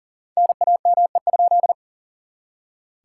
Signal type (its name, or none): Morse code